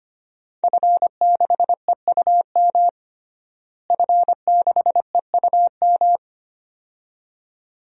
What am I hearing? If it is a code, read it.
Morse code, heard as a keyed tone, about 25 wpm, 700 Hz. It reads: F6EUM F6EUM
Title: DXCC-100 Call Signs - 5 Characters - Repeated 2x - 25wpm